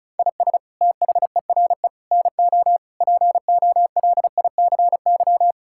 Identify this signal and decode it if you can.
Morse code, heard as a keyed tone, about 34 wpm, 700 Hz. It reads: IS THERE NO POLICY
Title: Sentences from Top 1000 Words - Review Part 5 - 35wpm